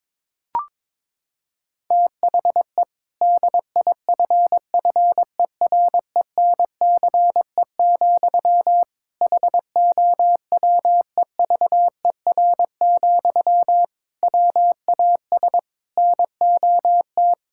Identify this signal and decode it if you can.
Morse code, heard as a keyed tone, about 22 wpm, 700 Hz. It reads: THE DIFFERENCE, HOWEVER, WAS NOT
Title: Sentences from Top 1000 Words - Review Part 5 - 22wpm